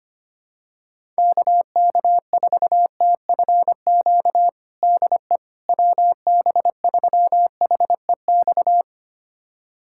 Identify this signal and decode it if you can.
Morse code, heard as a keyed tone, about 25 wpm, 700 Hz. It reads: KK4TFQ DE WB3HEX
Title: USA Callsign DE Callsign - 6 Characters - 25wpm